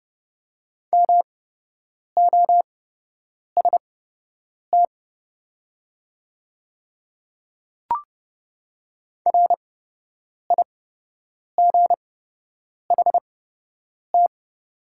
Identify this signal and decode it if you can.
Morse code, heard as a keyed tone, about 30 wpm, spaced as for 8 wpm, 700 Hz. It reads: MOST RIGHT
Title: 8x Character Spacing - Top 200 Words - 30wpm